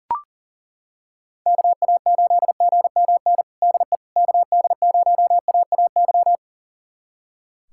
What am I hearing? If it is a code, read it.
Morse code, heard as a keyed tone, about 40 wpm, 700 Hz. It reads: KA8GMN DE KD0AAY
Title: USA Callsign DE Callsign 40wpm